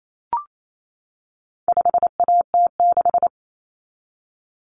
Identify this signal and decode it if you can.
Morse code, heard as a keyed tone, about 28 wpm, 700 Hz. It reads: 5AT6
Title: Common Cut Numbers - 4-Digit 28wpm